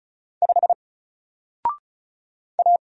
Morse code, heard as a keyed tone, about 34 wpm, 700 Hz. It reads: 5 A
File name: Single-Letter-Number-35wpm